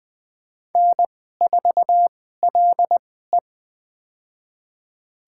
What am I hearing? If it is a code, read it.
Morse code, heard as a keyed tone, about 20 wpm, spaced as for 14 wpm, 700 Hz. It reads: N4LE
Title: USA Call Signs - 4 Characters - 2x Character Spacing - 20wpm